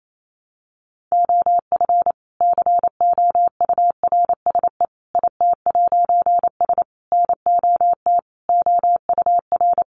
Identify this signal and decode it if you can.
Morse code, heard as a keyed tone, about 28 wpm, 700 Hz. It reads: OF COURSE IT'S NOT OUR
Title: A Sound Of Thunder 28wpm